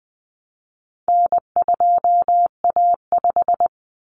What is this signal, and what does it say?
Morse code, heard as a keyed tone, about 20 wpm, 700 Hz. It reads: N2A5
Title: Common Cut Numbers - 4-Digit 20wpm